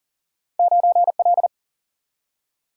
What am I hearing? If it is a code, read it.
Morse code, heard as a keyed tone, about 42 wpm, 700 Hz. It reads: 9L
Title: Single-Letter Single-Number 40wpm